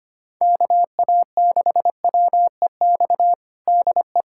Morse code, heard as a keyed tone, about 25 wpm, 700 Hz. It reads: KA6WEX DE
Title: USA Callsign DE Callsign - 6 Characters - 25wpm